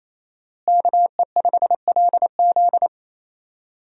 Morse code, heard as a keyed tone, about 28 wpm, 700 Hz. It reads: KE5LZ
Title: USA Call Signs - 5 Characters - Repeated 3x - 28wpm